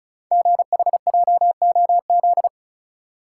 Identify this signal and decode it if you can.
Morse code, heard as a keyed tone, about 36 wpm, 690 Hz. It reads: GHJOZ
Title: Code Groups - 5 Letters - 35wpm